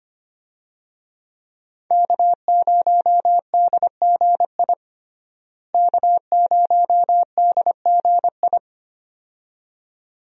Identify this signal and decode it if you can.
Morse code, heard as a keyed tone, about 25 wpm, 700 Hz. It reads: K0DGI K0DGI
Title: USA Call Signs - 5 Characters - Repeated 2x - 25wpm